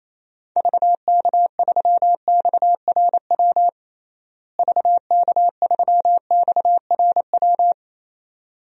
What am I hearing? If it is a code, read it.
Morse code, heard as a keyed tone, about 28 wpm, 700 Hz. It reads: VK3XRW VK3XRW
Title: DXCC-100 Call Signs - 6 Characters - Repeated 2x - 28wpm